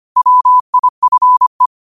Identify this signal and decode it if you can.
Morse code, heard as a keyed tone, about 25 wpm, 1000 Hz. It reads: WIFE
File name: qotdfiles / 25wpm